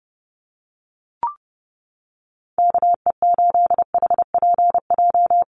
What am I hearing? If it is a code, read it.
Morse code, heard as a keyed tone, about 30 wpm, 700 Hz. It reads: KE8HPJ
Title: USA Call Signs Repeated 3x 30wpm